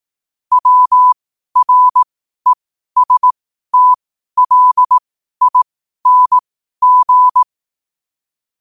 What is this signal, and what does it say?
Morse code, heard as a keyed tone, about 18 wpm, spaced as for 12 wpm, 1000 Hz. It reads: WRESTLING
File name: qotdfiles / 12wpm